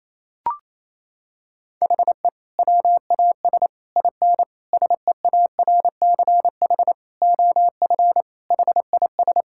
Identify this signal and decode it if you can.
Morse code, heard as a keyed tone, about 28 wpm, 700 Hz. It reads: HE WAS IN SEARCH OF HIS
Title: Sentences from Top 1000 Words - Review Part 5 - 28wpm